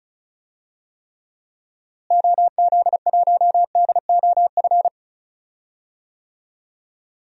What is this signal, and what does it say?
Morse code, heard as a keyed tone, about 36 wpm, 700 Hz. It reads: OZ1DOF
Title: DXCC-100 Call Signs - 6 Characters - Repeated 2x - 35wpm